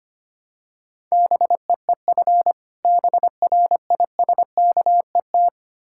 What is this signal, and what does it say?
Morse code, heard as a keyed tone, about 25 wpm, 700 Hz. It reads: BEEF BRISKET
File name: practice / 25wpm / Difficult Words with Hints - 25wpm